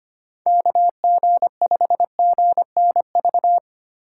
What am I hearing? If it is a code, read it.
Morse code, heard as a keyed tone, about 25 wpm, 700 Hz. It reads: KG5GNV
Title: USA Call Signs - 6 Characters - 3x Character Spacing - 25wpm